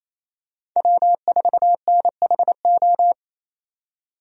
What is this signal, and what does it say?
Morse code, heard as a keyed tone, about 27 wpm, 700 Hz. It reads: W4NHO
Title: USA Call Signs - 5 Characters - 28wpm